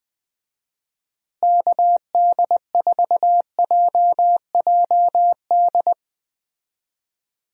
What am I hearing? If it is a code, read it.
Morse code, heard as a keyed tone, about 20 wpm, 700 Hz. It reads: KD4JJD
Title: USA Call Signs - 6 Characters - Repeated 2x - 20wpm